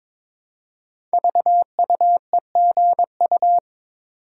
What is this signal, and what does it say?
Morse code, heard as a keyed tone, about 21 wpm, 700 Hz. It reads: VUEGU
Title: Code Groups - 5 Letters - 22wpm